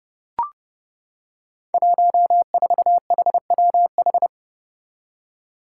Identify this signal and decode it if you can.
Morse code, heard as a keyed tone, about 30 wpm, 700 Hz. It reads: 14HWH
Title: Five Challenging Letter-Number Combinations 30wpm